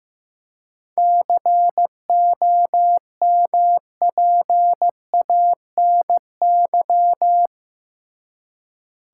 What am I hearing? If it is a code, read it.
Morse code, heard as a keyed tone, about 15 wpm, 700 Hz. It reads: COMPANY
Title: Seven Letter Words - 15wpm